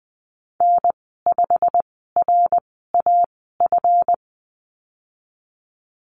Morse code, heard as a keyed tone, about 20 wpm, spaced as for 15 wpm, 700 Hz. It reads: N5RAF
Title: 161 Lesson 26g - Letter g - Call Signs 20wpm